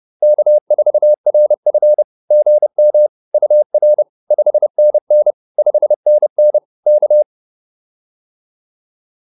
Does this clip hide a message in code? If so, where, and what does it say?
Yes: Morse code from 0.2 to 7.2 s, heard as a keyed tone, about 30 wpm, 600 Hz. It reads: K4RF GM UR 5NN 5NN K